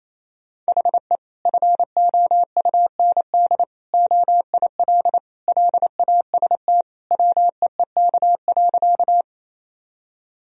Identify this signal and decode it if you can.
Morse code, heard as a keyed tone, about 28 wpm, 700 Hz. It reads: HE FOUND OIL LAST WEEK.
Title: Sentences from Top 1000 Words - Review Part 4 - 28wpm